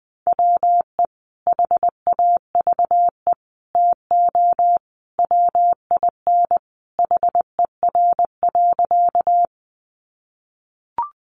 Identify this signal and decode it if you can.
Morse code, heard as a keyed tone, about 20 wpm, 700 Hz. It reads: WE HAVE TO WIN HER.
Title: Sentences from Top 1500 Words 20wpm